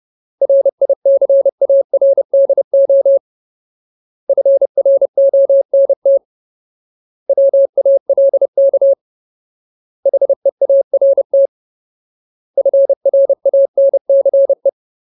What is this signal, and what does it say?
Morse code, heard as a keyed tone, about 30 wpm, 550 Hz. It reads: RICARDO FRONT WALK HEART FRANCE
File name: Ultimate Word List / 30WPM Ultimate